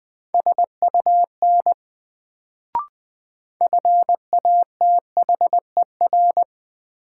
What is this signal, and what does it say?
Morse code, heard as a keyed tone, about 20 wpm, 700 Hz. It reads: SUN FATHER
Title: Sentences from Top 300 Words - New Word Drill - 20wpm